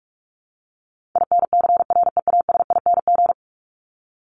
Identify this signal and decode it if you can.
Morse code, heard as a keyed tone, about 44 wpm, 700 Hz. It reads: INCREASING
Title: ING Suffix - Top 100 Words with Base Words 45wpm